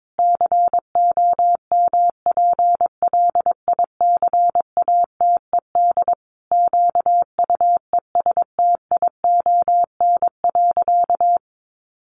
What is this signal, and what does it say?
Morse code, heard as a keyed tone, about 22 wpm, 700 Hz. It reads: COMPLICATED QUESTION.
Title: Sentences from Top 1200 Words 22wpm